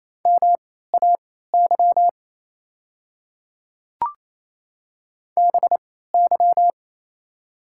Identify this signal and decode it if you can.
Morse code, heard as a keyed tone, about 27 wpm, spaced as for 16 wpm, 700 Hz. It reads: MAY BY
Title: 3x Character Spacing - Top 100 Words - 28wpm